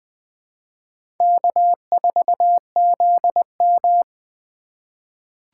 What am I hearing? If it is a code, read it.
Morse code, heard as a keyed tone, about 21 wpm, 700 Hz. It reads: K4ZM
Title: USA Call Signs - 4 Characters - 3x Character Spacing - 20wpm